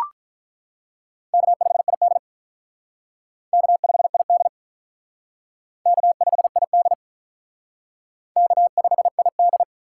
Morse code, heard as a keyed tone, about 42 wpm, 700 Hz. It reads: K5ID K5ID K5ID K5ID
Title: Speed Racing - USA Call Signs 35wpm